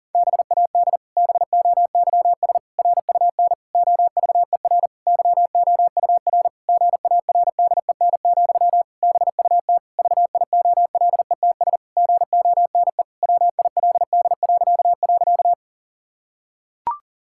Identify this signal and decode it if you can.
Morse code, heard as a keyed tone, about 40 wpm, 700 Hz. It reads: BAD BOYS RUN OVER YOUR GARDEN, BUT VIOLETS GONE WILD..
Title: Pangram Challenge v2 - 40wpm